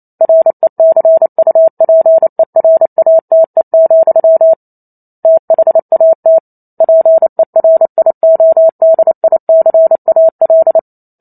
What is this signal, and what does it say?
Morse code, heard as a keyed tone, about 28 wpm, 650 Hz. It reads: RECUPERATE, THAT PERIODICAL